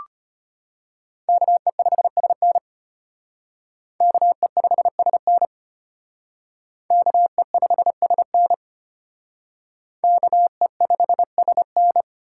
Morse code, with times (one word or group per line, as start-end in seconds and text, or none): 1.3-2.6 KE5SN
4.0-5.5 KE5SN
6.9-8.5 KE5SN
10.0-12.0 KE5SN